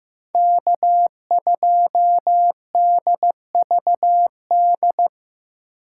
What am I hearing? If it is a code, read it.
Morse code, heard as a keyed tone, about 15 wpm, 700 Hz. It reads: K2DVD
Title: USA Call Signs - 5 Characters - 6x Character Spacing - 15wpm